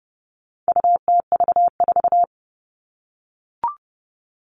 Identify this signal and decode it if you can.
Morse code, heard as a keyed tone, about 30 wpm, 700 Hz. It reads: UTV4
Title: All Cut Numbers - 4-Digit 30wpm